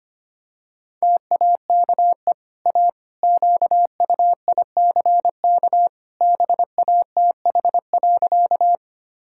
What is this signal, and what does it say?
Morse code, heard as a keyed tone, about 25 wpm, 700 Hz. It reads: TAKE A QUICK BATH.
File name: Easy Sentences with Hints - 25wpm